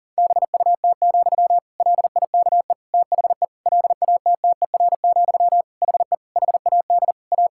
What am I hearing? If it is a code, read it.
Morse code, heard as a keyed tone, about 40 wpm, 700 Hz. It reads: BUT, LIKE THE LATTER, HE HAD A
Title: Sentences from Top 1600 Words 40wpm